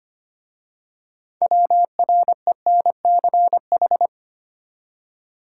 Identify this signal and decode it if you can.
Morse code, heard as a keyed tone, about 25 wpm, 700 Hz. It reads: WRENCH
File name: practice / 25wpm / Easy Words with Hints - 25wpm